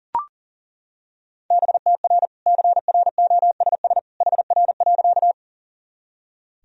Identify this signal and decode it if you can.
Morse code, heard as a keyed tone, about 40 wpm, 700 Hz. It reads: BTR CROSS HR.
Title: Sentences from Top 400 Words and Abbreviations 40wpm